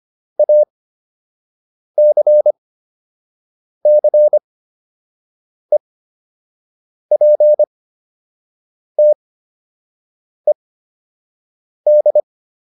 Morse code, heard as a keyed tone, about 25 wpm, spaced as for 6 wpm, 600 Hz. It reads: ACCEPTED
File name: ED8 06